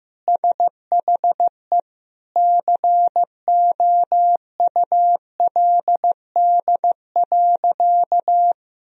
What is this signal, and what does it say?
Morse code, heard as a keyed tone, about 16 wpm, 700 Hz. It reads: SHE COULD.
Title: Sentences from Top 1000 Words - Review Part 3 - 15wpm